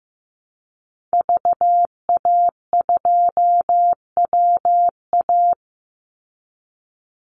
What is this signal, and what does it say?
Morse code, heard as a keyed tone, about 15 wpm, 700 Hz. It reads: VA2WA
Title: Call Signs Repeated 3x - 15wpm